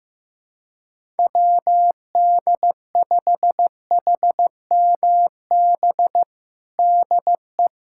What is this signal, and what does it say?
Morse code, heard as a keyed tone, about 15 wpm, 700 Hz. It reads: WD5HMB DE
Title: USA Callsign DE Callsign 15wpm